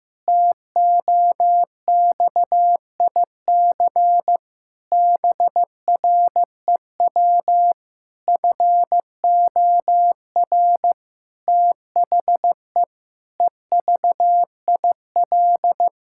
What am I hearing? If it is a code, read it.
Morse code, heard as a keyed tone, about 15 wpm, 700 Hz. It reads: TOXIC BREW FOR THE EVIL
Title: Pangram Challenge v2 - 15wpm